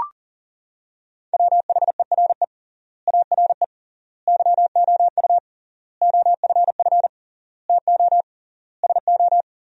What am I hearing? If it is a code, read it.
Morse code, heard as a keyed tone, about 37 wpm, 700 Hz. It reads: WHERE ARE YOU OFF TO SO
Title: 3x Word-Spacing - Sentences from Top 400 Words - 40wpm